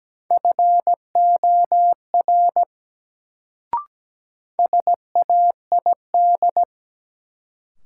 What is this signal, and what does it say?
Morse code, heard as a keyed tone, about 17 wpm, 700 Hz. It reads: FOR SAID
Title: Sentences from Top 100 Words - New Word Drill - 17wpm